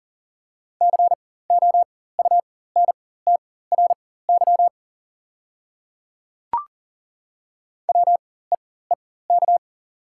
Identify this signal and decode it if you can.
Morse code, heard as a keyed tone, about 40 wpm, spaced as for 19 wpm, 700 Hz. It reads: COUNTRY WEEK
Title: 4x Character Spacing - Top 500 Words - 40wpm